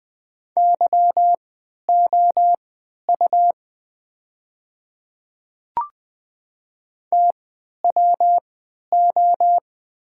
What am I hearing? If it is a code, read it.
Morse code, heard as a keyed tone, about 20 wpm, spaced as for 11 wpm, 700 Hz. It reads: YOU TWO
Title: 3x Character Spacing - Top 100 Words - 20wpm